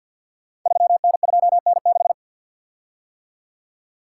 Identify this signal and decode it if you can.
Morse code, heard as a keyed tone, about 47 wpm, 690 Hz. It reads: 3N2N6
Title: Code Groups - 5 Letter:Numbers - 50wpm